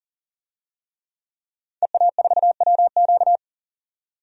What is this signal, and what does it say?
Morse code, heard as a keyed tone, about 40 wpm, 700 Hz. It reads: EA4WQ
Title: DXCC-100 Call Signs - 5 Characters - 40wpm